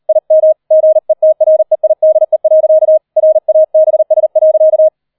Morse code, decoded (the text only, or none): IM GETREIDE. RABS.